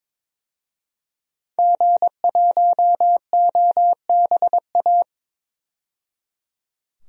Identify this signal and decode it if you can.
Morse code, heard as a keyed tone, about 22 wpm, 700 Hz. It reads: G1OBA
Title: DXCC-100 Call Signs - 5 Characters - 22wpm